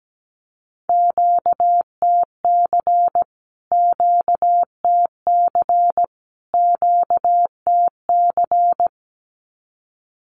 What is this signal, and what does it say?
Morse code, heard as a keyed tone, about 17 wpm, 700 Hz. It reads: QTC QTC QTC
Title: Q-Codes Repeated 3x 17wpm